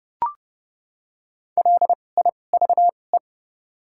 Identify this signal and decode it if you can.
Morse code, heard as a keyed tone, about 29 wpm, spaced as for 21 wpm, 700 Hz. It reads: LIVE
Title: 2x Character Spacing - Top 300 Words - 30wpm